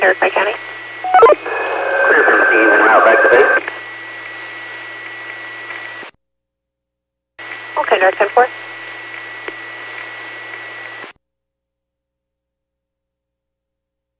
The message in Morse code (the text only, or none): none